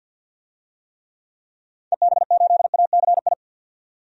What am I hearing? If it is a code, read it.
Morse code, heard as a keyed tone, about 49 wpm, 700 Hz. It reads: EB8ACI
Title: DXCC-100 Call Signs - 6 Characters - 50wpm